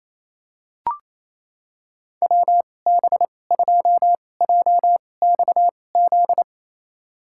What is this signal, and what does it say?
Morse code, heard as a keyed tone, about 28 wpm, spaced as for 20 wpm, 700 Hz. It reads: WB2JXZ
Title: USA Call Signs - 6 Characters - 2x Character Spacing - 28wpm